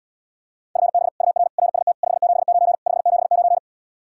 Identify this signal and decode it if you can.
Morse code, heard as a keyed tone, about 79 wpm, 700 Hz. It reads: FB UR RST 579 579